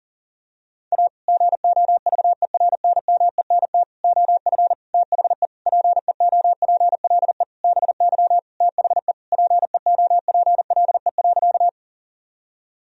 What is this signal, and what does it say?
Morse code, heard as a keyed tone, about 37 wpm, 700 Hz. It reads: A GOVERNMENT OF THE PEOPLE BY THE PEOPLE.
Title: Sentences from Top 1000 Words - Review Part 3 - 40wpm